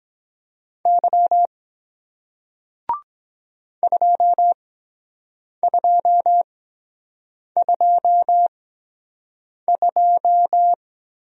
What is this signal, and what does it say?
Morse code, heard as a keyed tone, about 23 wpm, spaced as for 7 wpm, 700 Hz. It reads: Y 2222